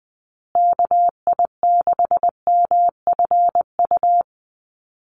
Morse code, heard as a keyed tone, about 20 wpm, 700 Hz. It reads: KI6MFU
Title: USA Call Signs Repeated 3x 20wpm